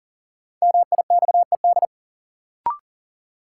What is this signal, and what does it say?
Morse code, heard as a keyed tone, about 40 wpm, 700 Hz. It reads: MIXED
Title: Sentences from Top 1500 Words - New Word Drill Alt - 40wpm